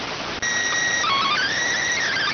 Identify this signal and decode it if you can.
SSTV (slow-scan television), PD 290.